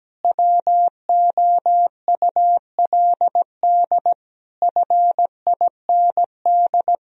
Morse code, heard as a keyed tone, about 17 wpm, 700 Hz. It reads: WOULD FIND
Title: Sentences from Top 1000 Words - Review Part 2 - 17wpm